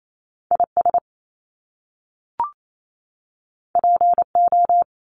Morse code, heard as a keyed tone, about 28 wpm, 700 Hz. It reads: IS PO
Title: Top Two Letter Combinations in Words 28wpm